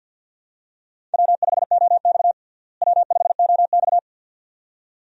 Morse code, heard as a keyed tone, about 51 wpm, 700 Hz. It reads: W5OX W5OX
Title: USA Call Signs - 4 Characters - Repeated 2x - 50wpm